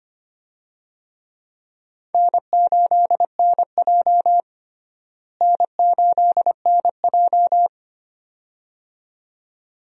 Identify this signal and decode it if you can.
Morse code, heard as a keyed tone, about 25 wpm, 700 Hz. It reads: N8NJ N8NJ
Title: USA Call Signs - 4 Characters - Repeated 2x - 25wpm